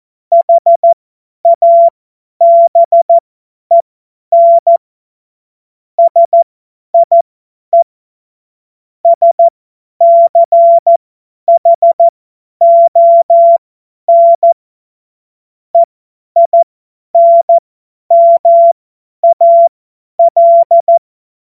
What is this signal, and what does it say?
Morse code, heard as a keyed tone, about 14 wpm, spaced as for 10 wpm, 680 Hz. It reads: HABEN SIE SCHON EINMAL
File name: Morsekurs / Witze / Witz 28.November 2023 50 F70